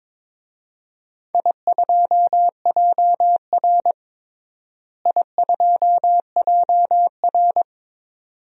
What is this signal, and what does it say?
Morse code, heard as a keyed tone, about 22 wpm, 700 Hz. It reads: I2JR I2JR
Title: DXCC-100 Call Signs - 4 Characters - Repeated 2x - 22wpm